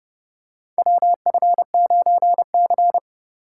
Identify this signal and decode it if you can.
Morse code, heard as a keyed tone, about 32 wpm, 700 Hz. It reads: WF9C
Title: USA Call Signs - 4x Character Spacing - 30wpm